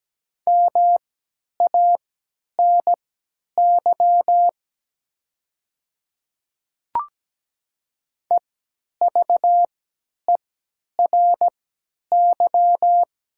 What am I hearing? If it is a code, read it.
Morse code, heard as a keyed tone, about 17 wpm, spaced as for 10 wpm, 700 Hz. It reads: MANY EVERY
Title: 3x Character Spacing - Top 300 Words - 17wpm